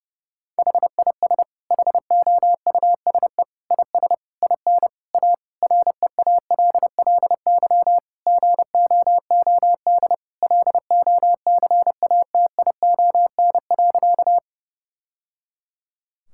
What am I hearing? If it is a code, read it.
Morse code, heard as a keyed tone, about 32 wpm, 700 Hz. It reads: HIS HOUSE IS IN A REALLY GOOD LOCATION.
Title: Sentences from Top 1000 Words - Review Part 6 - 30wpm